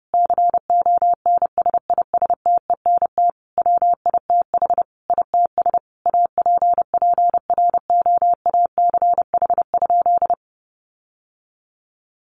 Morse code, heard as a keyed tone, about 30 wpm, 700 Hz. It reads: CONSISTENT WITH ITS APPROACH?
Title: Sentences from Top 1500 Words 30wpm